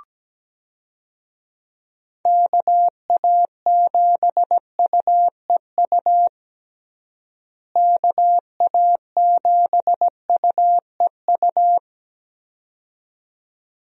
Morse code, heard as a keyed tone, about 17 wpm, 700 Hz. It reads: KA7UEU KA7UEU
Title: USA Call Signs - 6 Characters - Repeated 2x - 17wpm